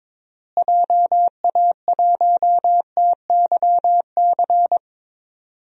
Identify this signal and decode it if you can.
Morse code, heard as a keyed tone, about 22 wpm, 700 Hz. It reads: JA1TYC
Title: DXCC-100 Call Signs - 6 Characters - Repeated 2x - 22wpm